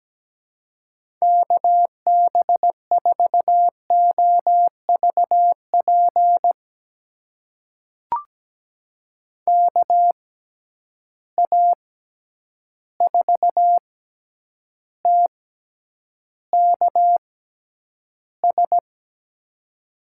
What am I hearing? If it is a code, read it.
Morse code, heard as a keyed tone, about 17 wpm, spaced as for 9 wpm, 700 Hz. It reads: KB4OVP KA4TKS
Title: USA Call Signs - 6 Characters - 6x Character Spacing - 17wpm